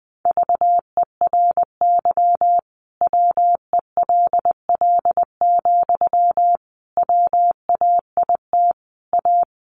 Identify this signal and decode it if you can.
Morse code, heard as a keyed tone, about 20 wpm, 700 Hz. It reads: VERY WELL, WAIT A
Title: Sentences from Top 500 Words 20wpm